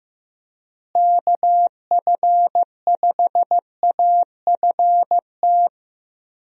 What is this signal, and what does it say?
Morse code, heard as a keyed tone, about 15 wpm, 700 Hz. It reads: KF5AFT
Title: USA Call Signs - 6 Characters - Repeated 3x - 15wpm